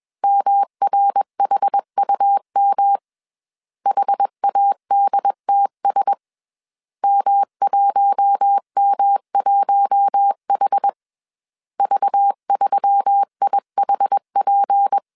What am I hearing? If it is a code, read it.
Morse code, heard as a keyed tone, about 21 wpm, 800 Hz. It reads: MRHUM HADTS M1M1H V3IHP
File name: CW / DET / 1 6 1500C S B 20wpm 2048